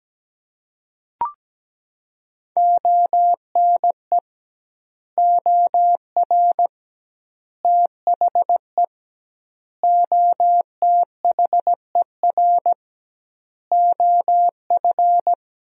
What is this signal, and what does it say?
Morse code, heard as a keyed tone, about 17 wpm, 700 Hz. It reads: ONE OR THE OTHER OF
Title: 2x Word-Spacing - Sentences from Top 400 Words - 17wpm